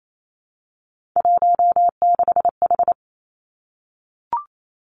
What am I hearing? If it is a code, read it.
Morse code, heard as a keyed tone, about 28 wpm, 690 Hz. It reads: 16H
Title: Three Challenging Letter-Number Combinations 28wpm